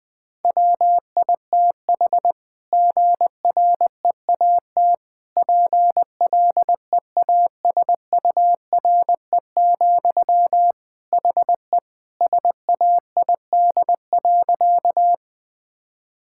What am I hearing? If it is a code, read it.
Morse code, heard as a keyed tone, about 20 wpm, 700 Hz. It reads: WITH GREAT PLEASURE, HE SAID.
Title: Sentences from Top 1600 Words 20wpm